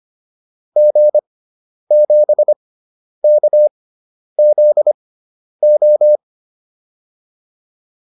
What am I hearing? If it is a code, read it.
Morse code, heard as a keyed tone, about 25 wpm, spaced as for 10 wpm, 600 Hz. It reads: G7KZO
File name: QSO404 10